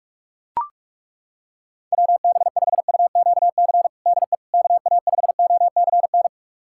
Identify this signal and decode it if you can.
Morse code, heard as a keyed tone, about 41 wpm, 700 Hz. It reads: WB5UQC DE KA5OCN
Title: USA Callsign DE Callsign - 6 Characters - 45wpm